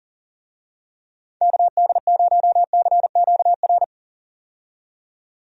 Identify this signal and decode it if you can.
Morse code, heard as a keyed tone, about 40 wpm, 700 Hz. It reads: KD0CQR
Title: USA Call Signs - 6 Characters - 40wpm